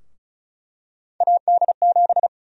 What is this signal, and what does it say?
Morse code, heard as a keyed tone, about 36 wpm, 700 Hz. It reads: AD7
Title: USA Call Sign Prefixes - Repeated 2x - 35wpm